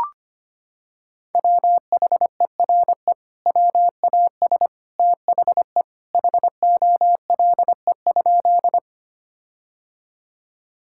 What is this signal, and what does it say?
Morse code, heard as a keyed tone, about 25 wpm, 700 Hz. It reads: WHERE WAS THE HOLE?
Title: Sentences from Top 1600 Words 25wpm